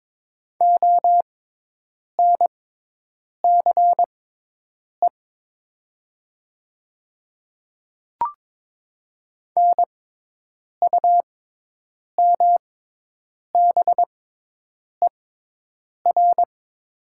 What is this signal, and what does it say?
Morse code, heard as a keyed tone, about 22 wpm, spaced as for 8 wpm, 700 Hz. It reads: ONCE NUMBER
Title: 6x Character Spacing - Top 300 Words - 22wpm